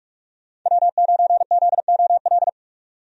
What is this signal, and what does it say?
Morse code, heard as a keyed tone, about 46 wpm, 700 Hz. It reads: W9ZOL
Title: USA Call Signs - 5 Characters - 45wpm